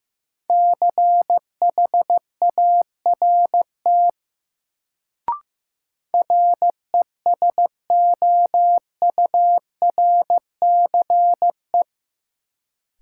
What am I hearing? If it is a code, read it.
Morse code, heard as a keyed tone, about 15 wpm, 700 Hz. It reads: CHART RESOURCE
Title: Sentences from Top 1300 Words - New Word Drill - 15wpm